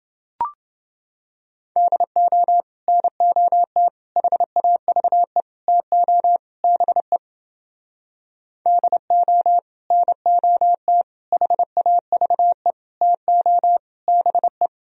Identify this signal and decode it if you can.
Morse code, heard as a keyed tone, about 30 wpm, 700 Hz. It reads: DO NOT HAVE TO BE DO NOT HAVE TO BE